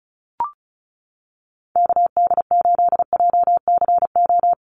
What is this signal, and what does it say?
Morse code, heard as a keyed tone, about 34 wpm, 700 Hz. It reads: KD8JCO
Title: USA Call Signs Repeated 3x 35wpm